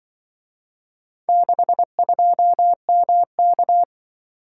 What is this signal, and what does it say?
Morse code, heard as a keyed tone, about 25 wpm, 700 Hz. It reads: 62MK
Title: Speed-Racing - Four Letter-Number Combinations - 20wpm